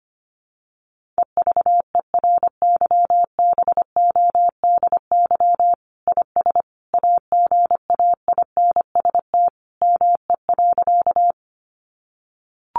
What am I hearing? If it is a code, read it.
Morse code, heard as a keyed tone, about 25 wpm, 700 Hz. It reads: EVERYBODY IS AGAINST ME.
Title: Sentences from Top 1400 Words 25wpm